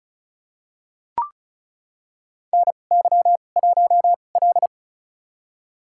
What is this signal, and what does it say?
Morse code, heard as a keyed tone, about 36 wpm, spaced as for 25 wpm, 700 Hz. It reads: NY1L
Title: USA Call Signs - 4 Characters - 2x Character Spacing - 35wpm